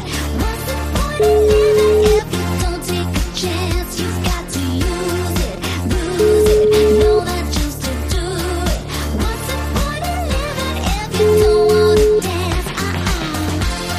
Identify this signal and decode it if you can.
Ringback tone (Europe, 425 Hz), with noise.